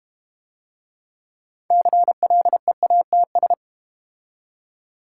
Morse code, heard as a keyed tone, about 32 wpm, 700 Hz. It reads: CLEATS